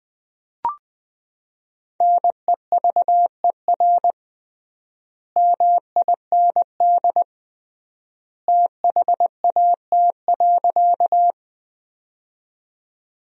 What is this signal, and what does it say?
Morse code, heard as a keyed tone, about 20 wpm, 700 Hz. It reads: NEVER MIND THAT.
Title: 3x Word-Spacing - Sentences from Top 500 Words - 20wpm